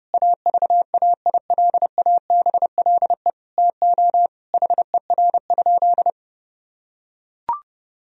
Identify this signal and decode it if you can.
Morse code, heard as a keyed tone, about 30 wpm, 700 Hz. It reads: AVAILABLE TO HER?
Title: Sentences from Top 1000 Words - Review Part 3 - 30wpm